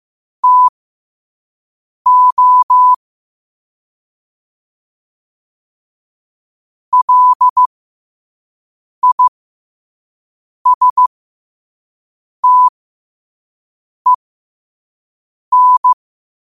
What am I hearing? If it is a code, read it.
Morse code, heard as a keyed tone, about 15 wpm, spaced as for 5 wpm, 1000 Hz. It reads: TO LISTEN
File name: qotdfiles / 5wpm